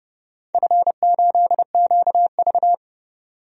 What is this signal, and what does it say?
Morse code, heard as a keyed tone, about 30 wpm, 700 Hz. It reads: F8QV
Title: DXCC-100 Call Signs - 4 Characters - 30wpm